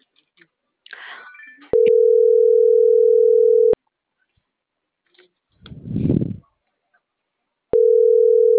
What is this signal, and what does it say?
Ringback tone (North America), 440 + 480 Hz.